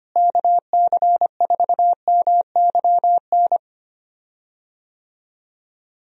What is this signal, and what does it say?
Morse code, heard as a keyed tone, about 25 wpm, 700 Hz. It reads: KC4MYN
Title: USA Call Signs - 6 Characters - Repeated 2x - 25wpm